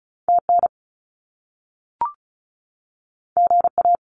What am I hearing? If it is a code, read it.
Morse code, heard as a keyed tone, about 35 wpm, 700 Hz. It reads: TN GA
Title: US States 35wpm